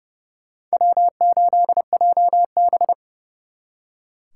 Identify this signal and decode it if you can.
Morse code, heard as a keyed tone, about 38 wpm, 700 Hz. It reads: W8JB